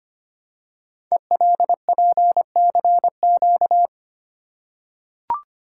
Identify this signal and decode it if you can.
Morse code, heard as a keyed tone, about 25 wpm, 700 Hz. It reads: ELPCQ
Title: Code Groups - 5 Letters - 25wpm